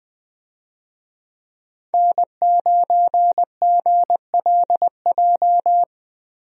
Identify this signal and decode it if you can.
Morse code, heard as a keyed tone, about 20 wpm, 700 Hz. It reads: N9GLJ
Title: USA Call Signs - 5 Characters - Repeated 3x - 20wpm